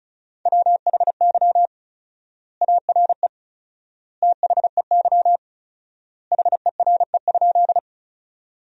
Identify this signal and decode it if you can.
Morse code, heard as a keyed tone, about 36 wpm, 700 Hz. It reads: WHY ARE THEY HERE?
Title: 4x Word-Spacing - Sentences from Top 200 Words - 35wpm